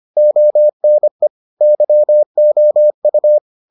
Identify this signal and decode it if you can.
Morse code, heard as a keyed tone, about 25 wpm, 600 Hz. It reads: ONE YOU